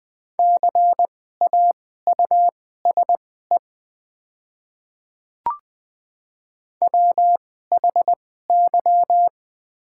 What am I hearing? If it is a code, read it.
Morse code, heard as a keyed tone, about 20 wpm, spaced as for 15 wpm, 700 Hz. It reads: CAUSE WHY
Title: 2x Character Spacing - Top 200 Words - 20wpm